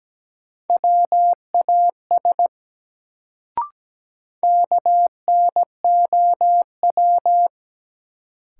Morse code, heard as a keyed tone, about 17 wpm, 700 Hz. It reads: WAS KNOW
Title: Sentences from Top 100 Words - New Word Drill - 17wpm